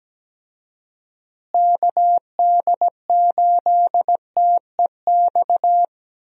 Morse code, heard as a keyed tone, about 17 wpm, 700 Hz. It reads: KD8TEX
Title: USA Call Signs - 6 Characters - Repeated 3x - 17wpm